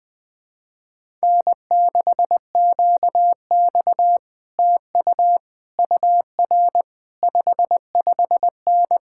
Morse code, heard as a keyed tone, about 20 wpm, 700 Hz. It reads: N6QX TU UR 55N
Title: Speed-Racing Common QSO Phrases - 20wpm